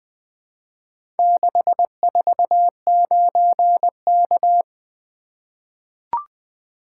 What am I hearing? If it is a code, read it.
Morse code, heard as a keyed tone, about 20 wpm, 700 Hz. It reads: 649K
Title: Four Letter-Number Combinations - 20wpm